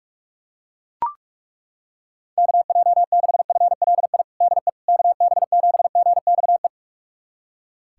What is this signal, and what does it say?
Morse code, heard as a keyed tone, about 41 wpm, 700 Hz. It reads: KJ6FLI DE KB7GXE
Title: USA Callsign DE Callsign - 6 Characters - 45wpm